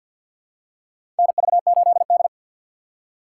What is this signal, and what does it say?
Morse code, heard as a keyed tone, about 52 wpm, 700 Hz. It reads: NV8D
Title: USA Call Signs - 4 Characters - 3x Character Spacing - 50wpm